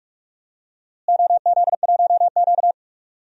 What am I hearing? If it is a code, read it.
Morse code, heard as a keyed tone, about 46 wpm, 700 Hz. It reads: OZ1Q
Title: DXCC-100 Call Signs - 4 Characters - Repeated 3x - 45wpm